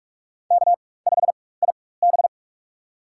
Morse code, heard as a keyed tone, about 44 wpm, spaced as for 21 wpm, 700 Hz. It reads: K5IB
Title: USA Call Signs - 4 Characters - 4x Character Spacing - 45wpm